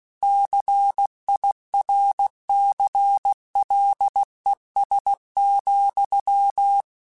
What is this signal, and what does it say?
Morse code, heard as a keyed tone, about 16 wpm, 800 Hz. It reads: CIRCLES,
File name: Technical Subjects / Technical Subjects-mp3s / XC @15